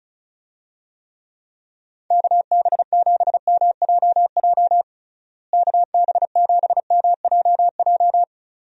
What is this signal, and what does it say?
Morse code, heard as a keyed tone, about 34 wpm, 700 Hz. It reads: KB7MJJ KB7MJJ
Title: USA Call Signs - 6 Characters - Repeated 3x - 35wpm